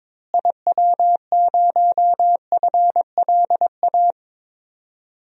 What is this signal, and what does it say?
Morse code, heard as a keyed tone, about 22 wpm, 700 Hz. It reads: IW0FLA
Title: DXCC-100 Call Signs - 6 Characters - Repeated 3x - 22wpm